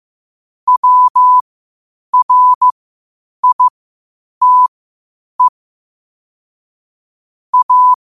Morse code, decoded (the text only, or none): WRITE A